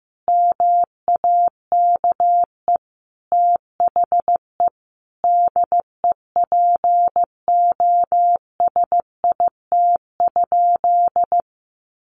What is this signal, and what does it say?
Morse code, heard as a keyed tone, about 15 wpm, 700 Hz. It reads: MAKE THE DEPOSIT?
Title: Sentences from Top 1300 Words 15wpm